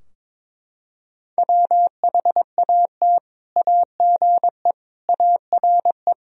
Morse code, heard as a keyed tone, about 22 wpm, 700 Hz. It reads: WHAT AGE ARE
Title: Sentences from Top 1000 Words - Review Part 3 - 22wpm